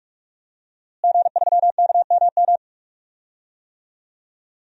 Morse code, heard as a keyed tone, about 47 wpm, 700 Hz. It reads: G3KMM
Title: DXCC-100 Call Signs - 5 Characters - Repeated 3x - 45wpm